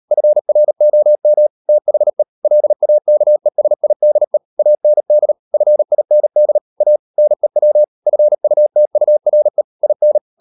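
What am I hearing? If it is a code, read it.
Morse code, heard as a keyed tone, about 38 wpm, 600 Hz. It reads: FROM THE LAKESIDE AND FIND A NEW FUTURE IN
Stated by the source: Morse code practice at 38 WPM, sending Ken Follett's novel Never in English, from CW med Gnister.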